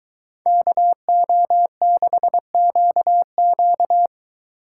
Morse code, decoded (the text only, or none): KO6QQ